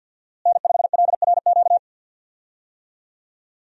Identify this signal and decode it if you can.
Morse code, heard as a keyed tone, about 52 wpm, 700 Hz. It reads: N5LRQ
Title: USA Call Signs - 5 Characters - Repeated 2x - 50wpm